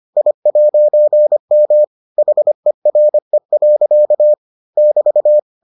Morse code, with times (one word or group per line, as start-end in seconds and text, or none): 0.2-1.8 I'M
2.2-4.3 HERE.
4.8-5.4 <BT>